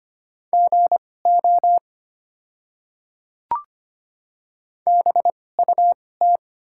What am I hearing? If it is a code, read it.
Morse code, heard as a keyed tone, about 25 wpm, spaced as for 18 wpm, 690 Hz. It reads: GO BUT
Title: 2x Character Spacing - Top 100 Words - 25wpm